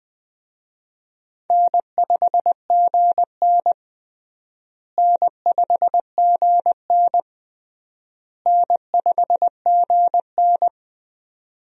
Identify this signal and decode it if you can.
Morse code, heard as a keyed tone, about 20 wpm, 700 Hz. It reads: N5GN N5GN N5GN
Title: USA Call Signs - 4 Characters - Repeated 3x - 20wpm